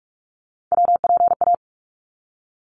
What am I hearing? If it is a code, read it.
Morse code, heard as a keyed tone, about 44 wpm, 700 Hz. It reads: WPA
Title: ARRL Sections 45wpm